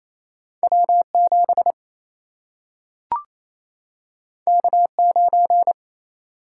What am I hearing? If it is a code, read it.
Morse code, heard as a keyed tone, about 28 wpm, 700 Hz. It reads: W7 K9
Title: Single-Letter Single-Number 28wpm